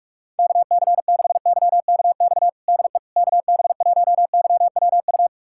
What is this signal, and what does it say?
Morse code, heard as a keyed tone, about 46 wpm, 690 Hz. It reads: KC6YKX DE KB1YWU
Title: USA Callsign DE Callsign - 6 Characters - 45wpm